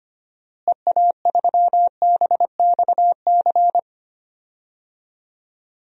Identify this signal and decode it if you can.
Morse code, heard as a keyed tone, about 25 wpm, 700 Hz. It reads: EA3BXC
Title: DXCC-100 Call Signs - 6 Characters - Repeated 3x - 25wpm